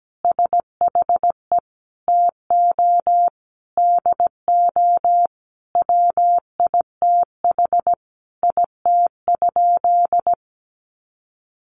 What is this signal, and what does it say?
Morse code, heard as a keyed tone, about 17 wpm, 700 Hz. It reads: SHE TO DO WITH IT?
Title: Sentences from Top 100 Words 17wpm